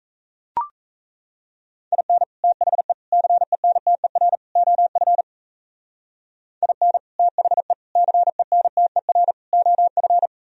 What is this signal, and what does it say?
Morse code, heard as a keyed tone, about 33 wpm, 700 Hz. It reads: IN THE CENTER OF IN THE CENTER OF
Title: Speed-Racing - Sets of 4 Words 28wpm